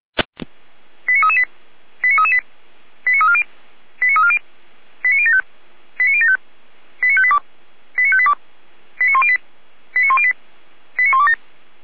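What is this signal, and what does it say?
Five-tone selective call (ZVEI), heard as quick runs of five tones, tones of about 70 ms. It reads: 89338 89338 89340 89340 89075 89075 89582 89582 89108 89108 89127
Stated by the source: Received on an SDR.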